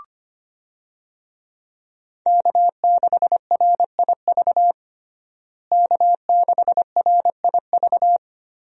Morse code, heard as a keyed tone, about 25 wpm, 700 Hz. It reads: K6RIV K6RIV
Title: USA Call Signs - 5 Characters - Repeated 3x - 25wpm